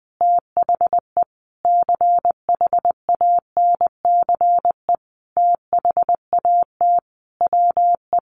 Morse code, heard as a keyed tone, about 20 wpm, 700 Hz. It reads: THE CHANCE THAT WE
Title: Sentences from Top 800 Words 20wpm